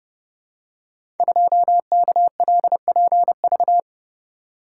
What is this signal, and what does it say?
Morse code, heard as a keyed tone, about 30 wpm, 700 Hz. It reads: 2KLPV
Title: Code Groups - 5 Letter:Numbers - 30wpm